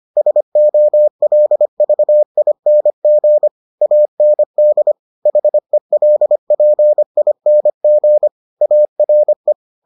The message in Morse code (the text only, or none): SOLVING AND HELPING ARE